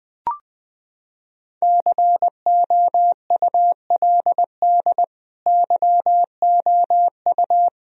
Morse code, heard as a keyed tone, about 20 wpm, 700 Hz. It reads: COULD YOU
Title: Sentences from Top 1000 Words - Review Part 2 - 20wpm